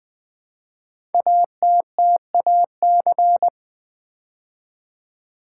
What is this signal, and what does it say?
Morse code, heard as a keyed tone, about 20 wpm, 700 Hz. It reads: ATTAC
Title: Build A Word - 5 Letters At A Time - 10-Letter Words - 20wpm